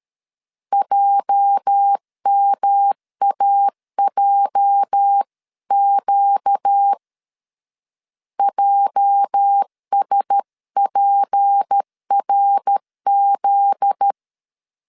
Morse code, heard as a keyed tone, about 13 wpm, 800 Hz. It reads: JMAJQ JSPRZ
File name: LEZIONI / 8 YJ45 / 8 YJ 12wpm 2235